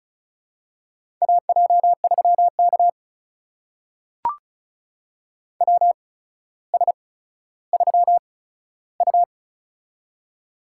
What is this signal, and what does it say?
Morse code, heard as a keyed tone, about 36 wpm, spaced as for 15 wpm, 700 Hz. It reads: AJ3K WS3U
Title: USA Call Signs - 4 Characters - 8x Character Spacing - 35wpm